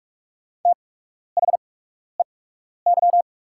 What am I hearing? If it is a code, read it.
Morse code, heard as a keyed tone, about 44 wpm, spaced as for 12 wpm, 690 Hz. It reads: THEY
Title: 8x Character Spacing - Top 100 Words - 45wpm